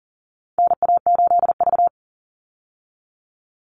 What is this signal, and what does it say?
Morse code, heard as a keyed tone, about 40 wpm, 700 Hz. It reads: NA8V
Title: Call Signs Repeated 3x - 40wpm